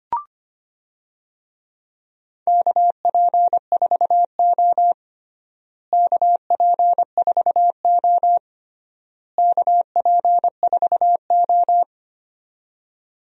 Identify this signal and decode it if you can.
Morse code, heard as a keyed tone, about 25 wpm, 700 Hz. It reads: KP4O KP4O KP4O
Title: DXCC-100 Call Signs - 4 Characters - Repeated 3x - 25wpm